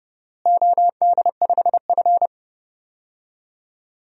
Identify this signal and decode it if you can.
Morse code, heard as a keyed tone, about 30 wpm, 700 Hz. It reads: OD5F
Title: DXCC-100 Call Signs - 4 Characters - Repeated 3x - 30wpm